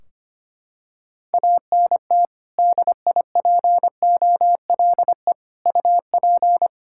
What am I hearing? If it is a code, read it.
Morse code, heard as a keyed tone, about 25 wpm, 700 Hz. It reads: ANT DIPOLE UP
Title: Practice QSO - Exchange 2 - Info - 25wpm